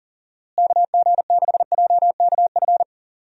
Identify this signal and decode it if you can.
Morse code, heard as a keyed tone, about 38 wpm, 700 Hz. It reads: KG6JKF
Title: USA Call Signs - 6 Characters - Repeated 2x - 40wpm